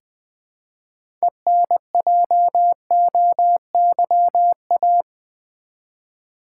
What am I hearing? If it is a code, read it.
Morse code, heard as a keyed tone, about 20 wpm, 700 Hz. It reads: ENJOYA
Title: Build A Word - 3 Letters At A Time - 9-Letter Words - 20wpm